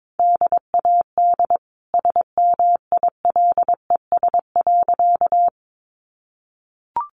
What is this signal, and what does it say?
Morse code, heard as a keyed tone, about 22 wpm, 700 Hz. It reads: DAD SMILES.
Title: Easy Phrases 22wpm